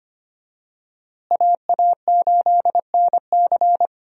Morse code, heard as a keyed tone, about 25 wpm, 700 Hz. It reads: AA8NC
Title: USA Call Signs - 5 Characters - Repeated 3x - 25wpm